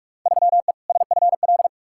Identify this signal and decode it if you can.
Morse code, heard as a keyed tone, about 46 wpm, 700 Hz. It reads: 3E SFL
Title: Field Day Exchanges Encore - 45wpm